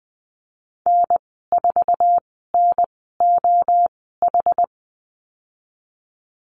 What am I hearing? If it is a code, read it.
Morse code, heard as a keyed tone, about 20 wpm, spaced as for 15 wpm, 700 Hz. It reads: N4NOH
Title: 231 Lesson 37g - Letter x - Call Signs 20wpm